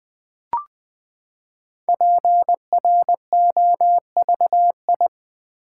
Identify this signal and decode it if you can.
Morse code, heard as a keyed tone, about 20 wpm, 700 Hz. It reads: PROVI
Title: Build A Word - 5 Letters At A Time - 10-Letter Words - 20wpm